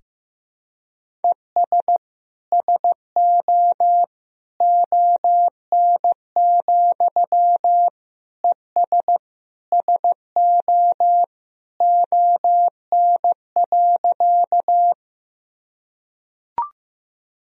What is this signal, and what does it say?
Morse code, heard as a keyed tone, about 16 wpm, 700 Hz. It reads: ES SO ON, ES SO ON.
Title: Sentences from Top 100 Words and Abbreviations 15wpm